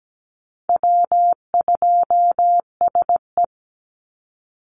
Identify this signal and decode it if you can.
Morse code, heard as a keyed tone, about 17 wpm, 700 Hz. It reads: W2SE
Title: NA SOTA Chaser Callsigns 17wpm